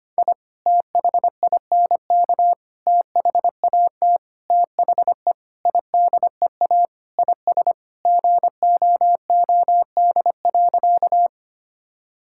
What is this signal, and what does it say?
Morse code, heard as a keyed tone, about 25 wpm, 700 Hz. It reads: I THINK THAT THE IDEA IS GOOD.
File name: practice / 25wpm / Sentences with 4-Word Phrases - 25wpm